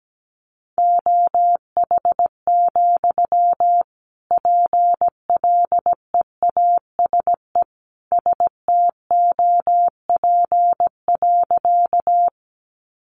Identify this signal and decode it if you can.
Morse code, heard as a keyed tone, about 17 wpm, 700 Hz. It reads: OH, PLEASE STOP.
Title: Sentences from Top 1500 Words 17wpm